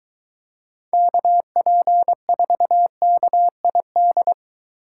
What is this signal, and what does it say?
Morse code, heard as a keyed tone, about 23 wpm, 700 Hz. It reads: KP4KID